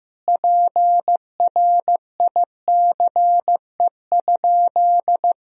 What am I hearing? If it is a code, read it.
Morse code, heard as a keyed tone, about 16 wpm, 700 Hz. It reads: PRICE?
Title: Sentences from Top 1000 Words - Review Part 4 - 15wpm